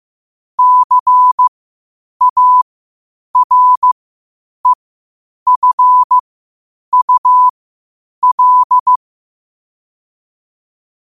Morse code, heard as a keyed tone, about 15 wpm, spaced as for 8 wpm, 1000 Hz. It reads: CAREFUL
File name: qotdfiles / 8wpm